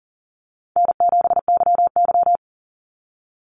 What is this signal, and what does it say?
Morse code, heard as a keyed tone, about 40 wpm, 700 Hz. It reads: N7YY
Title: Call Signs Repeated 2x - 40wpm